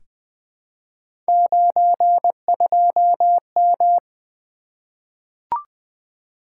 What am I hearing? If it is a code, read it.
Morse code, heard as a keyed tone, about 20 wpm, 700 Hz. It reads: 92M
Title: Three Letter-Number Combinations - 20wpm